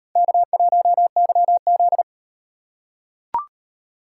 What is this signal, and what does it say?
Morse code, heard as a keyed tone, about 37 wpm, 700 Hz. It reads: K1YZ